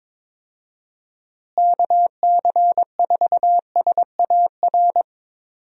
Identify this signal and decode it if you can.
Morse code, heard as a keyed tone, about 22 wpm, 700 Hz. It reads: KC4SAR
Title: USA Call Signs - 6 Characters - Repeated 2x - 22wpm